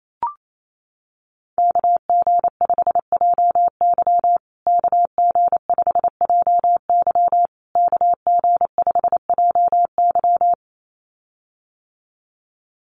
Morse code, heard as a keyed tone, about 28 wpm, 700 Hz. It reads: KG5JY KG5JY KG5JY
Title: USA Call Signs Repeated 3x 28wpm